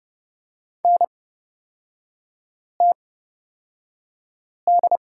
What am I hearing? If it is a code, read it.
Morse code, heard as a keyed tone, about 30 wpm, spaced as for 5 wpm, 700 Hz. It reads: NTD
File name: Lesson 36e - BT - Three Characters 30wpm